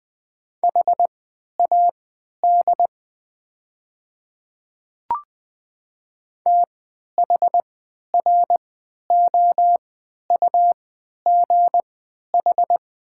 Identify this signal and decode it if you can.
Morse code, heard as a keyed tone, about 21 wpm, spaced as for 11 wpm, 700 Hz. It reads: HAD THROUGH
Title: 3x Character Spacing - Top 200 Words - 20wpm